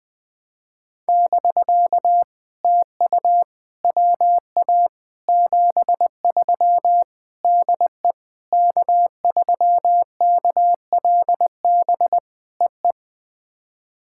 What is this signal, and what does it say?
Morse code, heard as a keyed tone, about 20 wpm, 700 Hz. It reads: <BK> TU WA 73 DE K3KLB EE
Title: POTA QSO Practice - 20wpm